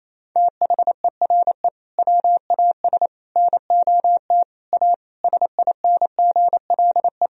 Morse code, heard as a keyed tone, about 28 wpm, 700 Hz. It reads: THERE WAS NOT A SINGLE
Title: Sentences from Top 1000 Words - Review Part 4 - 28wpm